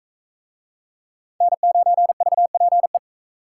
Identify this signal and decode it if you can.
Morse code, heard as a keyed tone, about 38 wpm, 700 Hz. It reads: N9VPE